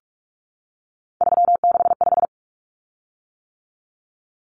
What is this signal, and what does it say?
Morse code, heard as a keyed tone, about 44 wpm, 700 Hz. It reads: 365
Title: Common Cut Numbers - 3-Digit 45wpm